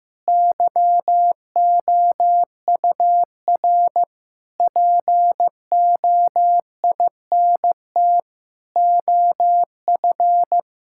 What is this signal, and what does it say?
Morse code, heard as a keyed tone, about 15 wpm, 700 Hz. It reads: YOUR POINT OF
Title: Sentences from Top 1000 Words - Review Part 6 - 15wpm